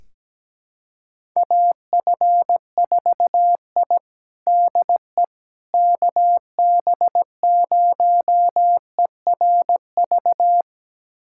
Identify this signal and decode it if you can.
Morse code, heard as a keyed tone, about 17 wpm, 700 Hz. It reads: AF4I DE KB0ERV
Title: USA Callsign DE Callsign 17wpm